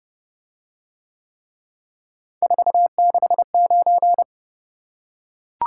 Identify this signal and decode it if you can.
Morse code, heard as a keyed tone, about 28 wpm, 700 Hz. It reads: 469